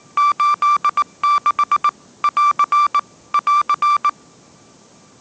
Morse code, heard as a keyed tone, about 19 wpm, 1170 Hz. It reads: 86 <AR> <AR>